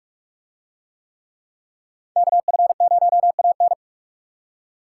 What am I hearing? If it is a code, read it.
Morse code, heard as a keyed tone, about 42 wpm, 700 Hz. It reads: KF0AN